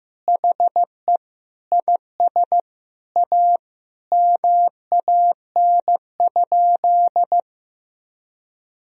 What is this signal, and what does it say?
Morse code, heard as a keyed tone, about 15 wpm, 700 Hz. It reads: HE IS A MAN?
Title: Sentences from Top 1000 Words - Review Part 3 - 15wpm